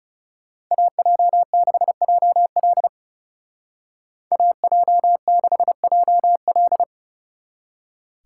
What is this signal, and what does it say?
Morse code, heard as a keyed tone, about 36 wpm, 700 Hz. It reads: AJ6JL AJ6JL
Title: Speed Racing - USA 2x2 Call Signs 30wpm